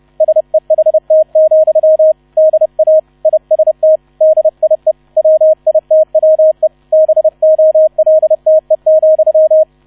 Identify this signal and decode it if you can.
Morse code, heard as a keyed tone, about 30 wpm, 630 Hz, with radio noise. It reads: SEHT, DA IST DIE WITWE BOLTE,